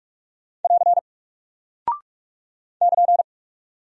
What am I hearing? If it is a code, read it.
Morse code, heard as a keyed tone, about 44 wpm, 700 Hz. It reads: <AR> <KN>
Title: Single Character (Letter:Number:Punctuation:Prosign) Alt 45wpm